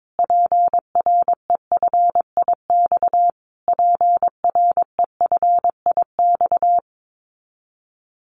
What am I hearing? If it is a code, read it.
Morse code, heard as a keyed tone, about 22 wpm, 700 Hz. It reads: PREFIX PREFIX
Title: Callsign Prefixes for Top 25 DXCC - Repeated 3x - 22wpm